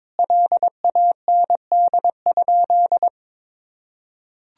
Morse code, heard as a keyed tone, about 21 wpm, 700 Hz. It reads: LAND?
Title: Sentences from Top 1000 Words - Review Part 1 - 22wpm